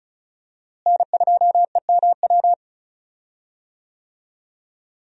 Morse code, heard as a keyed tone, about 36 wpm, 700 Hz. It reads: N2EMW
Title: USA Call Signs - 5 Characters - Repeated 2x - 35wpm